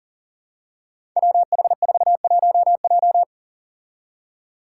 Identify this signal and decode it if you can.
Morse code, heard as a keyed tone, about 42 wpm, 700 Hz. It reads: WH41J
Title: Five Challenging Letter-Number Combinations 40wpm